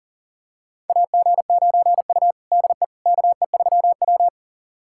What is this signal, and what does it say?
Morse code, heard as a keyed tone, about 40 wpm, 700 Hz. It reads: AG9U DE KE3W
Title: USA Callsign DE Callsign - 4 Characters - 40wpm